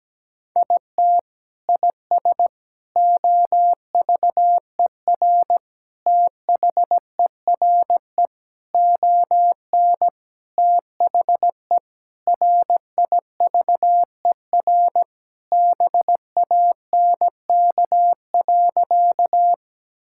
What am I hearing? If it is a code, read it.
Morse code, heard as a keyed tone, about 17 wpm, 700 Hz. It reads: IT IS OVER THERE ON THE RIVER BANK.
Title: Sentences from Top 1000 Words - Review Part 6 - 17wpm